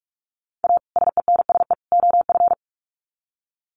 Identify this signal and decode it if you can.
Morse code, heard as a keyed tone, about 44 wpm, 700 Hz. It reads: A SENSE OF
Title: Sets of 3 Words 45wpm